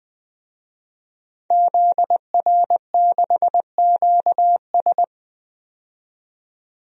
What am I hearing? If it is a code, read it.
Morse code, heard as a keyed tone, about 20 wpm, 700 Hz. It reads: ZR6QS
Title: DXCC-100 Call Signs - 5 Characters - 20wpm